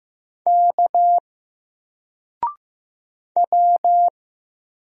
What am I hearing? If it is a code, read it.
Morse code, heard as a keyed tone, about 15 wpm, 700 Hz. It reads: K W
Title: Single Character (Letter:Number:Punctuation:Prosign) 15wpm